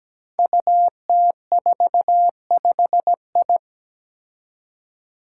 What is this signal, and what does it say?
Morse code, heard as a keyed tone, about 17 wpm, 700 Hz. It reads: UT45I
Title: Five Challenging Letter-Number Combinations 17wpm